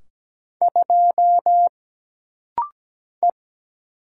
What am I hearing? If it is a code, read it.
Morse code, heard as a keyed tone, about 17 wpm, 700 Hz. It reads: 2 E
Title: Single Letter-Number Alt - Rapid-Fire 17wpm